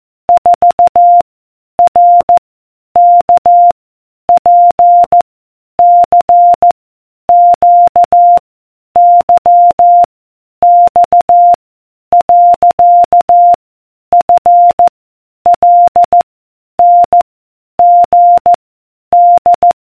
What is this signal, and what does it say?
Morse code, heard as a keyed tone, about 14 wpm, 700 Hz. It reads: 4 R K P C Q Y X . F L N G D